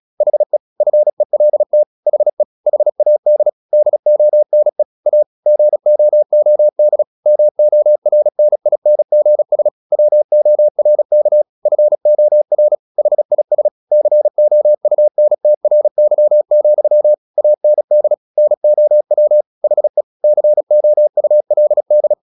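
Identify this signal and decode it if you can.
Morse code, heard as a keyed tone, about 36 wpm, 600 Hz. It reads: HE FELT HE HAD DONE A GOOD MORNINGS WORK FOR HIS COUNTRY, AND NOW HE COULD
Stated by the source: Morse practice at 36 wpm, text of Ken Follett's novel Never, from CW med Gnister.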